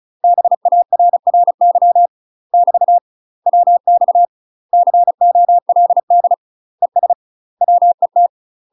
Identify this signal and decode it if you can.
Morse code, heard as a keyed tone, about 36 wpm, 700 Hz. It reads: BARRY <BT> WX COLD ES WET
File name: UK MASTER QSOs TRAINING / 35WPM / UK MASTER QSOs 0025